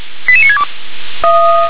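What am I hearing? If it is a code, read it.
Five-tone selective call (ZVEI), heard as quick runs of five tones, tones of about 70 ms. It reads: 88962
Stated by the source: ZVEI five-tone call 88645 received on an SDR.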